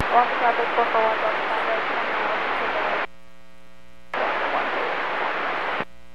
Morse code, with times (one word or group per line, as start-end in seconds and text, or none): none